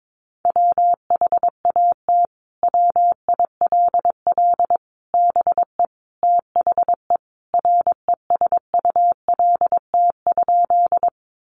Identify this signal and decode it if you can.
Morse code, heard as a keyed tone, about 22 wpm, 700 Hz. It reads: WHAT WILL BE THE RESULT?
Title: Sentences from Top 700 Words 22wpm